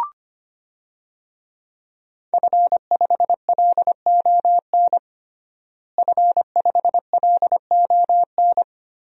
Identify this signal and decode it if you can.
Morse code, heard as a keyed tone, about 25 wpm, 700 Hz. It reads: F5LON F5LON
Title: DXCC-100 Call Signs - 5 Characters - Repeated 2x - 25wpm